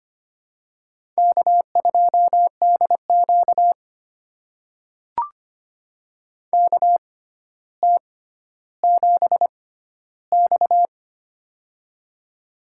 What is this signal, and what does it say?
Morse code, heard as a keyed tone, about 25 wpm, spaced as for 13 wpm, 700 Hz. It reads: K2DQ KT7X
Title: USA Call Signs - 4 Characters - 6x Character Spacing - 25wpm